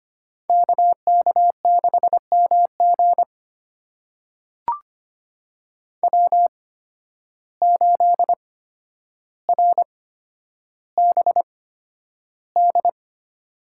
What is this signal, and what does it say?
Morse code, heard as a keyed tone, about 25 wpm, spaced as for 11 wpm, 700 Hz. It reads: KK6MG W8RBD
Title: USA Call Signs - 5 Characters - 8x Character Spacing - 25wpm